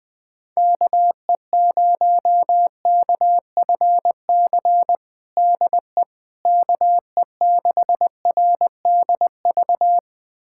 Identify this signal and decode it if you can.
Morse code, heard as a keyed tone, about 21 wpm, 700 Hz. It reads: KE0KFC DE KE6RDV
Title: USA Callsign DE Callsign - 6 Characters - 20wpm